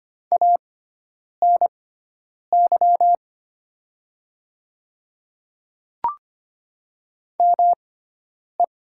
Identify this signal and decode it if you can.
Morse code, heard as a keyed tone, about 25 wpm, spaced as for 9 wpm, 700 Hz. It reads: ANY ME
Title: 6x Character Spacing - Top 300 Words - 25wpm